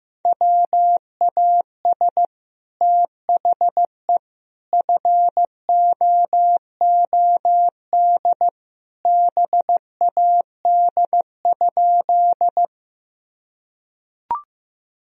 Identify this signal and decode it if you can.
Morse code, heard as a keyed tone, about 15 wpm, 700 Hz. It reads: WAS THE FOOD BAD?
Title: Sentences from Top 1000 Words - Review Part 1 - 15wpm